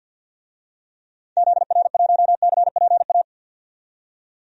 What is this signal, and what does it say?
Morse code, heard as a keyed tone, about 52 wpm, 700 Hz. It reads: ZR1CPA
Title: DXCC-100 Call Signs - 6 Characters - 50wpm